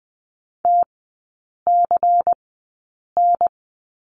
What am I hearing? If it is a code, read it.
Morse code, heard as a keyed tone, about 20 wpm, spaced as for 8 wpm, 700 Hz. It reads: TCN
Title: 107 Lesson 18e - Letter m - Three Characters 20wpm